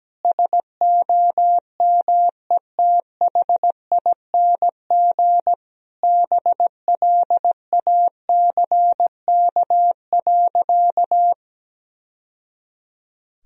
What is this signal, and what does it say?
Morse code, heard as a keyed tone, about 17 wpm, 700 Hz. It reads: SOMETHING BLACK.
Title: Sentences from Top 1000 Words - Review Part 2 - 17wpm